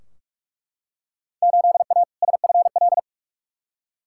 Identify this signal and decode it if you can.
Morse code, heard as a keyed tone, about 42 wpm, 700 Hz. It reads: 8A SFL
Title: Field Day Exchanges Encore - 45wpm